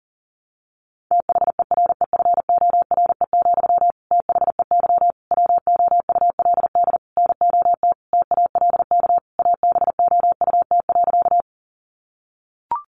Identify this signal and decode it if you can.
Morse code, heard as a keyed tone, about 40 wpm, 700 Hz. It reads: THEREFORE, THEY WOULD NOT TALK ABOUT.
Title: Sentences from Top 800 Words 40wpm